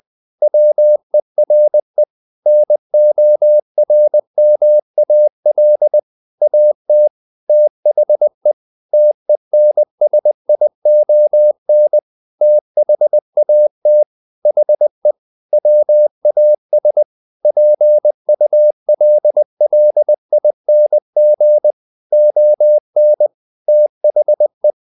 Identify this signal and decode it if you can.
Morse code, heard as a keyed tone, about 20 wpm, 600 Hz. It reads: WERE NORMAL AT THE TENSION THAT HE WAS PULLING ON THE
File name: the old man and the sea part 2 20wpm